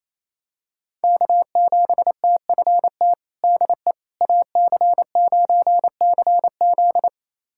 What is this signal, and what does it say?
Morse code, heard as a keyed tone, about 28 wpm, 700 Hz. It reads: K7TFT DE AC9CZ
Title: USA Callsign DE Callsign - 5 Characters - 28wpm